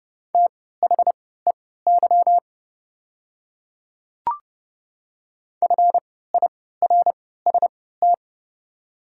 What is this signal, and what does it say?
Morse code, heard as a keyed tone, about 30 wpm, spaced as for 17 wpm, 700 Hz. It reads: THEY FIRST
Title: 3x Character Spacing - Top 500 Words - 30wpm